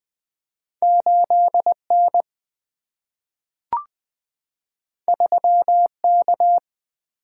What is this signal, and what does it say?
Morse code, heard as a keyed tone, about 20 wpm, 700 Hz. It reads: 8N 3K
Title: Single-Letter Single-Number 20wpm